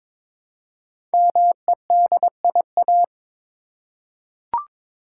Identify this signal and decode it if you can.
Morse code, heard as a keyed tone, about 22 wpm, 700 Hz. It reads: MEDIA
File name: 5 Letter Words - First 3 Letters Given - 22wpm